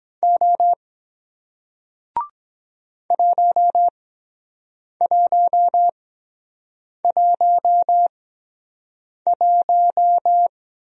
Morse code, heard as a keyed tone, about 22 wpm, spaced as for 7 wpm, 700 Hz. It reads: O 1111